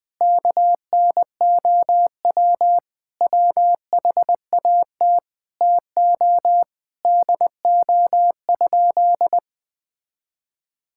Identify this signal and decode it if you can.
Morse code, heard as a keyed tone, about 20 wpm, 700 Hz. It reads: KNOW WHAT TO DO?
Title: Sentences with 2-Word Phrases - Spoken Beginning - 20wpm